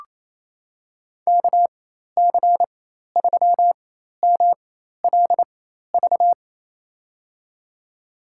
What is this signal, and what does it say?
Morse code, heard as a keyed tone, about 27 wpm, spaced as for 13 wpm, 700 Hz. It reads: KC3MLV
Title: USA Call Signs - 6 Characters - 4x Character Spacing - 28wpm